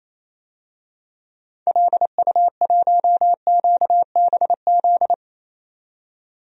Morse code, heard as a keyed tone, about 28 wpm, 700 Hz. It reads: LU1QBZ
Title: DXCC-100 Call Signs - 6 Characters - 28wpm